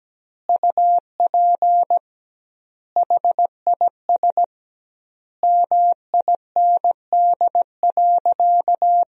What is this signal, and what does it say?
Morse code, heard as a keyed tone, about 17 wpm, 700 Hz. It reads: UP HIS MIND.
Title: 2x Word-Spacing - Sentences from Top 500 Words - 17wpm